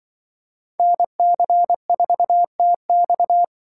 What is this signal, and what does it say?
Morse code, heard as a keyed tone, about 25 wpm, 700 Hz. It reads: NC4TX
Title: Speed Racing - USA 2x2 Call Signs 20wpm